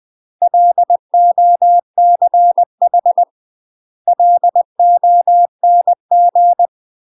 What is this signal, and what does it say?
Morse code, heard as a keyed tone, about 20 wpm, 700 Hz. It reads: LOCH LONG
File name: UK MASTER QSOs TRAINING / 20WPM / UK MASTER QSOs 0013